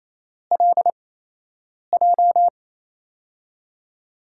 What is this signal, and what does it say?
Morse code, heard as a keyed tone, about 28 wpm, 700 Hz. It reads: L J
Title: USA Call Signs - 5 Characters - 8x Character Spacing - 28wpm